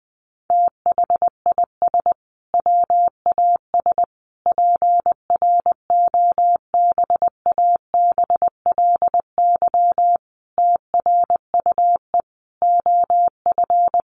Morse code, heard as a keyed tone, about 20 wpm, 700 Hz. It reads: THIS WAS PROBABLY TRUE OF
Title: Sentences from Top 700 Words 20wpm